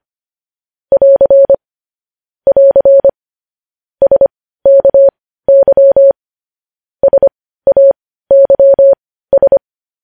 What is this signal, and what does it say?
Morse code, heard as a keyed tone, about 25 wpm, spaced as for 15 wpm, 550 Hz. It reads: <AR> <AR> SKY SAYS
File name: podcast / News headlines-1x-25x15